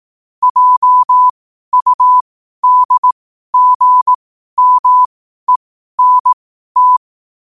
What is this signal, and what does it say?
Morse code, heard as a keyed tone, about 18 wpm, spaced as for 12 wpm, 1000 Hz. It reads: JUDGMENT
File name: qotdfiles / 12wpm